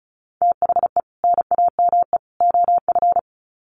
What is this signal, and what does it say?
Morse code, heard as a keyed tone, about 34 wpm, 700 Hz. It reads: THE NAME OF
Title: Sets of 3 Words 35wpm